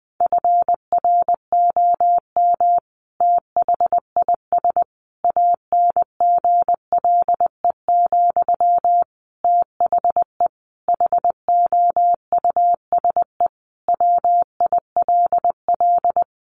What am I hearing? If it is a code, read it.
Morse code, heard as a keyed tone, about 20 wpm, 700 Hz. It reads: FROM THIS ANGLE, THE HOUSE WILL
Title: Sentences from Top 1500 Words 20wpm